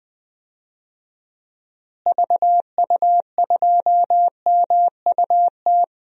Morse code, heard as a keyed tone, about 21 wpm, 700 Hz. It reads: VU2MUT
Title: DXCC-100 Call Signs - 6 Characters - Repeated 2x - 20wpm